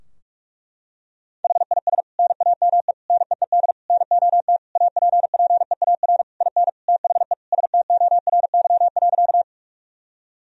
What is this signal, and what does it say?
Morse code, heard as a keyed tone, about 46 wpm, 700 Hz. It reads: HIS NAME NEED NOT APPEAR IN THE STORY.
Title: Sentences from Top 1000 Words - Review Part 2 - 45wpm